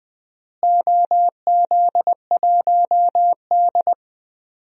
Morse code, heard as a keyed tone, about 20 wpm, 700 Hz. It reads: OZ1D
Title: DXCC-100 Call Signs - 4 Characters - Repeated 2x - 20wpm